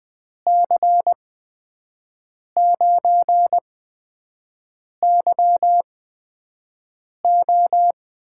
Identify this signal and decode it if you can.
Morse code, heard as a keyed tone, about 20 wpm, spaced as for 5 wpm, 700 Hz. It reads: C9YO
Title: USA Call Signs - 6 Characters - 8x Character Spacing - 20wpm